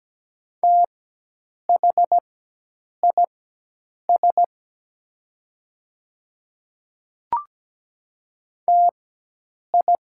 Morse code, heard as a keyed tone, about 17 wpm, spaced as for 8 wpm, 700 Hz. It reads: THIS TI
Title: 4x Character Spacing - Top 100 Words - 17wpm